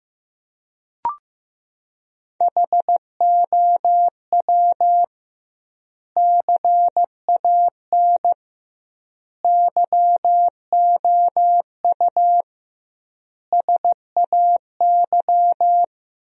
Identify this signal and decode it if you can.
Morse code, heard as a keyed tone, about 16 wpm, 700 Hz. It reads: HOW CAN YOU SAY
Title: 2x Word-Spacing - Sentences from Top 200 Words - 15wpm